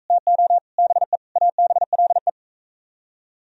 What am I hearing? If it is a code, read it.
Morse code, heard as a keyed tone, about 40 wpm, 700 Hz. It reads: TO BE ABLE